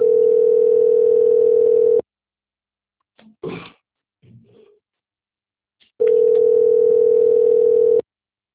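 Ringback tone (North America), 440 + 480 Hz.
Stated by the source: Outbound telephone call recorded on the Sao Roque system.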